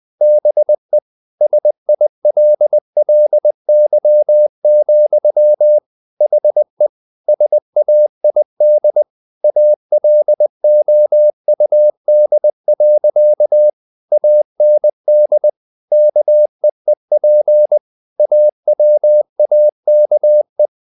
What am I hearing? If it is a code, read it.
Morse code, heard as a keyed tone, about 20 wpm, 600 Hz. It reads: BE SILLY, HE SAID ALOUD. AND KEEP AWAKE
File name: the old man and the sea part 2 20wpm